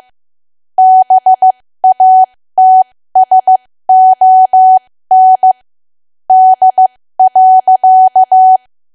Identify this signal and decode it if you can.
Morse code, heard as a keyed tone, about 15 wpm, spaced as for 13 wpm, 750 Hz. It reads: BATSON D.
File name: cwa / W1AW-Level2 / 13WPM